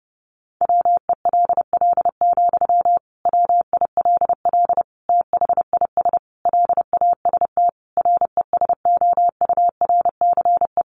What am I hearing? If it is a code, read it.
Morse code, heard as a keyed tone, about 30 wpm, 700 Hz. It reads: WELL, WILL THIS LAST RESOURCE
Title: Sentences from Top 1300 Words 30wpm